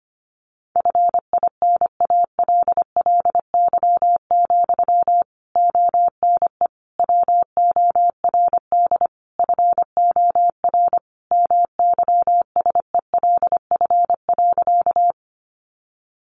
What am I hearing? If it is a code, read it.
Morse code, heard as a keyed tone, about 25 wpm, 700 Hz. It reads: FINALLY, ONE WORD FOR MYSELF.
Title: Sentences from Top 1300 Words 25wpm